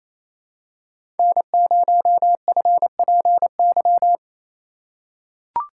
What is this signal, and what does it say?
Morse code, heard as a keyed tone, about 28 wpm, 700 Hz. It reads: N0FPY
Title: USA Call Signs - 5 Characters - 2x Character Spacing - 28wpm